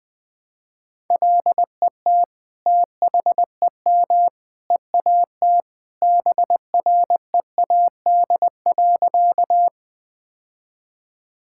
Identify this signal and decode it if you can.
Morse code, heard as a keyed tone, about 20 wpm, 700 Hz. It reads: LET THEM EAT BREAD.
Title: Sentences from Top 1600 Words 20wpm